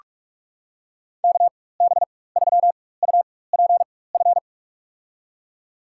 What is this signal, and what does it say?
Morse code, heard as a keyed tone, about 48 wpm, spaced as for 21 wpm, 690 Hz. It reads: KB3UPF
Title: USA Call Signs - 4x Character Spacing - 45wpm